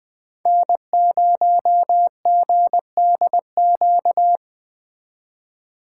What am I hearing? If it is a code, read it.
Morse code, heard as a keyed tone, about 19 wpm, 700 Hz. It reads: N0GDQ